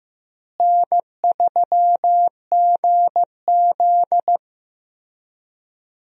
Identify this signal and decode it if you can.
Morse code, heard as a keyed tone, about 15 wpm, 700 Hz. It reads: N3GZ
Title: USA Call Signs - 4 Characters - Repeated 3x - 15wpm